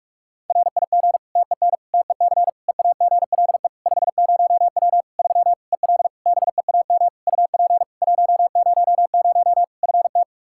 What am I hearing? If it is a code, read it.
Morse code, heard as a keyed tone, about 46 wpm, 700 Hz. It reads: RIG TEN TEC EAGLE 50W 3 EL BEAM UP 100 FT
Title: Practice QSO - Exchange 2 - Info - 45wpm